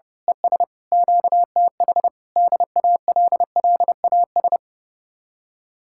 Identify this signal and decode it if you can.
Morse code, heard as a keyed tone, about 30 wpm, 700 Hz. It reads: ES QTH DALLAS
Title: Speed-Racing Common QSO Phrases Encore - 30wpm